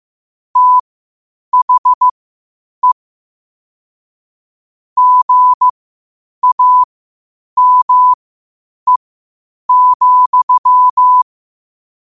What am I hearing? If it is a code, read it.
Morse code, heard as a keyed tone, about 15 wpm, spaced as for 8 wpm, 1000 Hz. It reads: THE GAME,
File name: qotdfiles / 8wpm